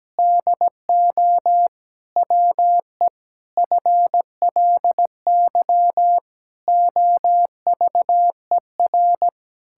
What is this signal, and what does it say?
Morse code, heard as a keyed tone, about 17 wpm, 700 Hz. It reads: DO WE FLY OVER
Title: Pangram Challenge v2 - 17wpm